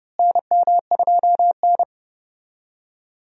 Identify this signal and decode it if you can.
Morse code, heard as a keyed tone, about 30 wpm, 700 Hz. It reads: NM2N
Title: USA Call Signs - 4 Characters - Repeated 3x - 30wpm